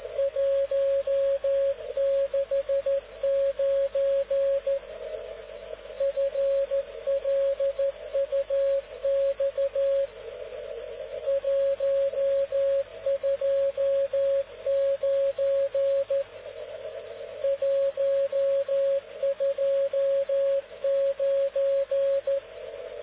Morse code, heard as a keyed tone, about 13 wpm, 550 Hz. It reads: 169 FLUX 129 129